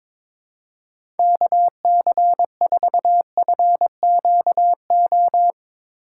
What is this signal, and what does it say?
Morse code, heard as a keyed tone, about 23 wpm, 700 Hz. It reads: KC4FQO
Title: USA Call Signs - 6 Characters - 3x Character Spacing - 22wpm